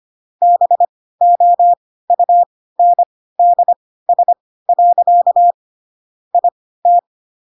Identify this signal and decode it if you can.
Morse code, heard as a keyed tone, about 25 wpm, spaced as for 16 wpm, 700 Hz. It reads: BOUNDS. IT